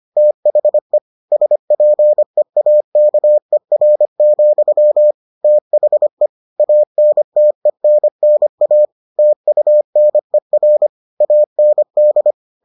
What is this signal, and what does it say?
Morse code, heard as a keyed tone, about 25 wpm, 600 Hz. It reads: THE SPEAKER, THE ANTENNA TUNER AND